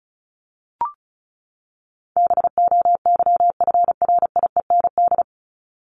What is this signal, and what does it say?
Morse code, heard as a keyed tone, about 34 wpm, 700 Hz. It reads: BOYFRIEND
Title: Top 501-750 Words 35wpm